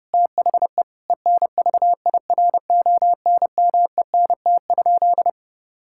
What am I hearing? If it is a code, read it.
Morse code, heard as a keyed tone, about 32 wpm, 700 Hz. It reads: THE ENVIRONMENT?
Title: Sentences from Top 1000 Words - Review Part 6 - 30wpm